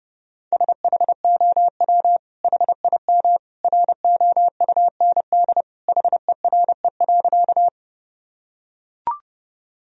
Morse code, heard as a keyed tone, about 32 wpm, 700 Hz. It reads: SHOW HIM ROUND HERE.
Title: Sentences from Top 1000 Words - Review Part 1 - 30wpm